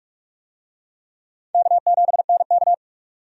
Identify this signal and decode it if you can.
Morse code, heard as a keyed tone, about 46 wpm, 700 Hz. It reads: K7NK
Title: USA Call Signs - 4 Characters - 45wpm